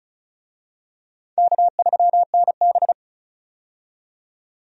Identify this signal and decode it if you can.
Morse code, heard as a keyed tone, about 36 wpm, 700 Hz. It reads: K3NB
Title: USA Call Signs - 4 Characters - 35wpm